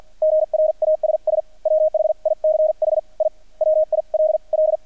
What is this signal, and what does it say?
Morse code, heard as a keyed tone, about 50 wpm, spaced as for 42 wpm, 620 Hz. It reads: GRASS WHICH I WILL
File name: archived / CRSnet